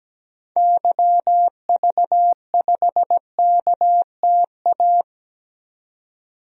Morse code, heard as a keyed tone, about 17 wpm, 700 Hz. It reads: YV5KTA
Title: DXCC-100 Call Signs - 6 Characters - 17wpm